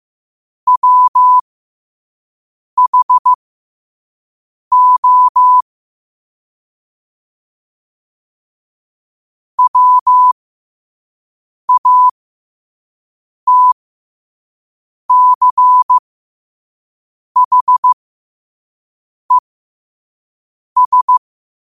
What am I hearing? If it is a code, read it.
Morse code, heard as a keyed tone, about 15 wpm, spaced as for 5 wpm, 1000 Hz. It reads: WHO WATCHES
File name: qotdfiles / 5wpm